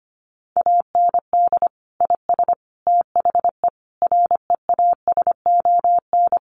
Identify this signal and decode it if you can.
Morse code, heard as a keyed tone, about 25 wpm, 700 Hz. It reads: AND IS THE REASON
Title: Sentences from Top 700 Words 25wpm